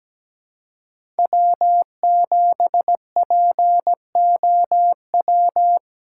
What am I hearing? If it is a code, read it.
Morse code, heard as a keyed tone, about 17 wpm, 700 Hz. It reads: W7POW
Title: USA Call Signs - 5 Characters - 3x Character Spacing - 17wpm